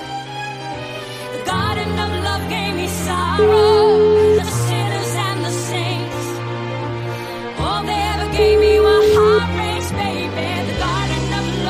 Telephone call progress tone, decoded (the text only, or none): ringback tone